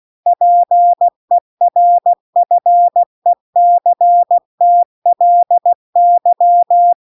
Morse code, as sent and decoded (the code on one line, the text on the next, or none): .--. .-. ..-. -.-. - .-.. -.--
PERFECTLY